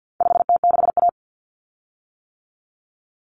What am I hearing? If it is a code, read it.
Morse code, heard as a keyed tone, about 50 wpm, 700 Hz. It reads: 5T6A
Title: Common Cut Numbers - 4-Digit 50wpm